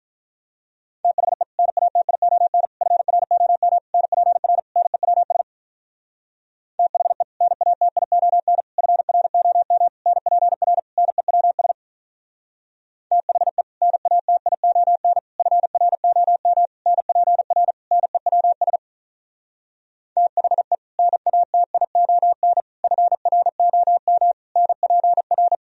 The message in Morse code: THE NATION FROM NPR NEWS THE NATION FROM NPR NEWS THE NATION FROM NPR NEWS THE NATION FROM NPR

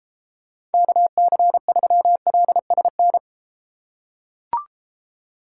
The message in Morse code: KC3LSN

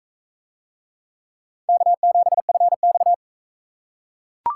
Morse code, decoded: K7FX

